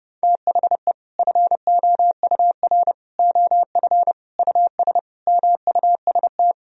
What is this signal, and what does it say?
Morse code, heard as a keyed tone, about 32 wpm, 700 Hz. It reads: THE FOUR OF US MUST
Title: Sentences from Top 1000 Words - Review Part 1 - 30wpm